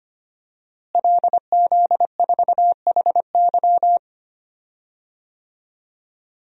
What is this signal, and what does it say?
Morse code, heard as a keyed tone, about 25 wpm, 700 Hz. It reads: LZ4HY
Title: DXCC-100 Call Signs - 5 Characters - Repeated 3x - 25wpm